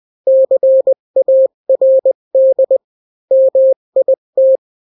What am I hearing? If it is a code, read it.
Morse code, heard as a keyed tone, about 20 wpm, 530 Hz. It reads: CARD MIT
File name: daily / 18wpm